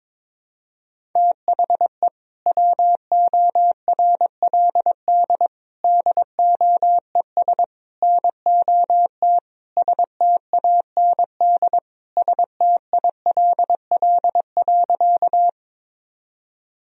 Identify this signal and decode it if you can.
Morse code, heard as a keyed tone, about 22 wpm, 700 Hz. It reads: THE WORLD DOES NOT STAND STILL.
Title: Sentences from Top 1000 Words - Review Part 1 - 22wpm